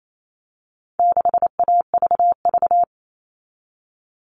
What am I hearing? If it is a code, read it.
Morse code, heard as a keyed tone, about 28 wpm, 700 Hz. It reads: 6AVV
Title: All Cut Numbers - 4-Digit 28wpm